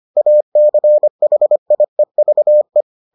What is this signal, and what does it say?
Morse code, heard as a keyed tone, about 25 wpm, 600 Hz. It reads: ACHIEVE